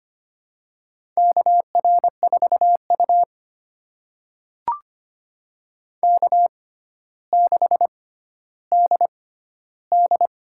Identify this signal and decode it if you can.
Morse code, heard as a keyed tone, about 25 wpm, spaced as for 13 wpm, 700 Hz. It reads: KR4U K6DD
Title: USA Call Signs - 4 Characters - 6x Character Spacing - 25wpm